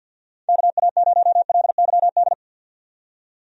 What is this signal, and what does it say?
Morse code, heard as a keyed tone, about 51 wpm, 700 Hz. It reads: KA0LYD
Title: USA Call Signs - 6 Characters - 50wpm